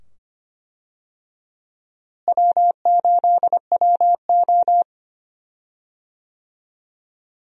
Morse code, heard as a keyed tone, about 25 wpm, 700 Hz. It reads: W8WO
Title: USA Call Signs - 4 Characters - Repeated 2x - 25wpm